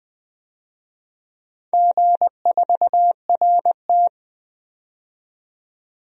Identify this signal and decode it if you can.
Morse code, heard as a keyed tone, about 20 wpm, 700 Hz. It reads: G4RT
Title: DXCC-100 Call Signs - 4 Characters - 20wpm